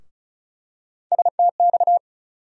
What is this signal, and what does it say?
Morse code, heard as a keyed tone, about 36 wpm, 700 Hz. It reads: STX
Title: USA Call Sign Postfixes - 3 Letters 35wpm